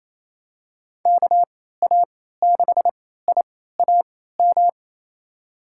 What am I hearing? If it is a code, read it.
Morse code, heard as a keyed tone, about 28 wpm, spaced as for 16 wpm, 700 Hz. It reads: KA6IAM
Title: USA Call Signs - 6 Characters - 3x Character Spacing - 28wpm